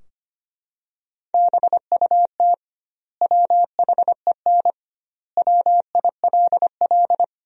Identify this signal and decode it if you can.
Morse code, heard as a keyed tone, about 25 wpm, 700 Hz. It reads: BUT WHEN WILL
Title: 2x Word-Spacing - Sentences from Top 100 Words - 25wpm